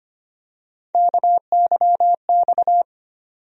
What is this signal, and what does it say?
Morse code, heard as a keyed tone, about 25 wpm, 700 Hz. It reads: KYX
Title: USA Call Sign Postfixes - 3 Letters 25wpm